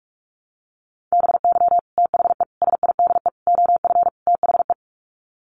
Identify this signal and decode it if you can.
Morse code, heard as a keyed tone, about 44 wpm, 700 Hz. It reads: BY THE SIDE OF THE
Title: Sets of 5 Words Encore2 45wpm